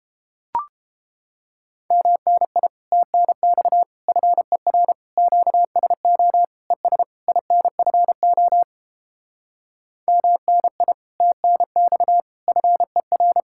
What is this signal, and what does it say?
Morse code, heard as a keyed tone, about 29 wpm, 700 Hz. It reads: MNI TNX FER QSO ES INFO MNI TNX FER